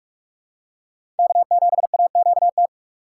Morse code, heard as a keyed tone, about 46 wpm, 700 Hz. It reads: K7AQT
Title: USA Call Signs - 5 Characters - 4x Character Spacing - 45wpm